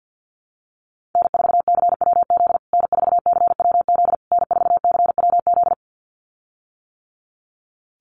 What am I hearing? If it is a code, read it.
Morse code, heard as a keyed tone, about 50 wpm, 700 Hz. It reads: N4CWZ N4CWZ N4CWZ
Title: USA Call Signs Repeated 3x 50wpm